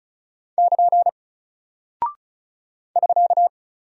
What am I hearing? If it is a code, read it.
Morse code, heard as a keyed tone, about 36 wpm, 700 Hz. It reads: <KN> <SK>
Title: Single Character (Letter:Number:Punctuation:Prosign) Alt 35wpm